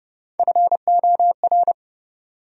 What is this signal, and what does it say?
Morse code, heard as a keyed tone, about 40 wpm, 700 Hz. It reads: FOR